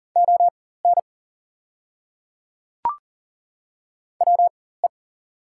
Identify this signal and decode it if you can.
Morse code, heard as a keyed tone, about 40 wpm, spaced as for 19 wpm, 690 Hz. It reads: ON WE
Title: 4x Character Spacing - Top 100 Words - 40wpm